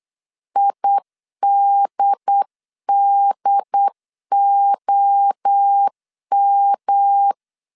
Morse code, heard as a keyed tone, about 8 wpm, 800 Hz. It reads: IDDOM